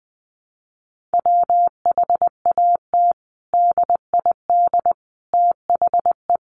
Morse code, heard as a keyed tone, about 20 wpm, 700 Hz. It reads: WHAT DID THE
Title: Sentences from Top 500 Words 20wpm